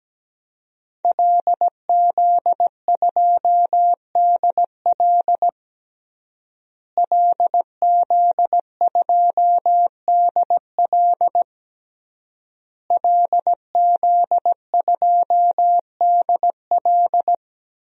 Morse code, heard as a keyed tone, about 17 wpm, 700 Hz. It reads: LZ2DL LZ2DL LZ2DL
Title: DXCC-100 Call Signs - 5 Characters - Repeated 3x - 17wpm